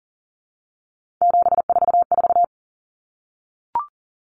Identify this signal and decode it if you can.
Morse code, heard as a keyed tone, about 40 wpm, 700 Hz. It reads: 744
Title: Numbers - 3-Digit 40wpm